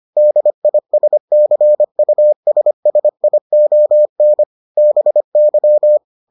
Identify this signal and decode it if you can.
Morse code, heard as a keyed tone, about 25 wpm, 600 Hz. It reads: DISCUSSION BY